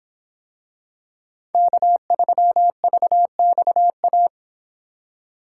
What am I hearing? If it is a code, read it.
Morse code, heard as a keyed tone, about 26 wpm, 700 Hz. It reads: K3VXA